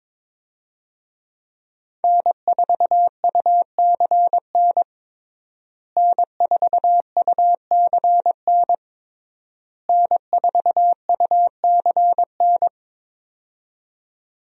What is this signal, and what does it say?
Morse code, heard as a keyed tone, about 22 wpm, 700 Hz. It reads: N4UCN N4UCN N4UCN
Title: USA Call Signs - 5 Characters - Repeated 3x - 22wpm